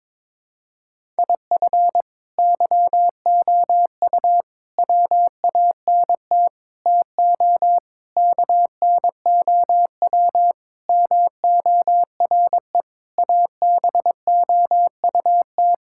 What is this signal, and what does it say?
Morse code, heard as a keyed tone, about 22 wpm, 700 Hz. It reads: IF YOU WANT TO KNOW MORE ABOUT
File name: practice / 22wpm / Sets of 8 Words - 22wpm